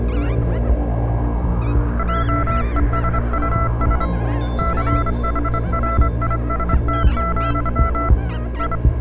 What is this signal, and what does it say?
Morse code, heard as a keyed tone, about 25 wpm, 1340 Hz, with noise. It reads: JESUS CHRIST, I